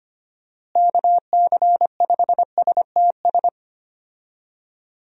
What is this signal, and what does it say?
Morse code, heard as a keyed tone, about 25 wpm, 700 Hz. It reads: KC5STS
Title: USA Call Signs - 6 Characters - Repeated 3x - 25wpm